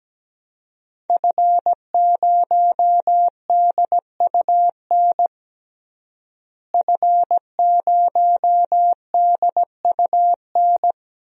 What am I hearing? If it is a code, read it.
Morse code, heard as a keyed tone, about 17 wpm, 700 Hz. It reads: F0DUN F0DUN
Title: DXCC-100 Call Signs - 5 Characters - Repeated 2x - 17wpm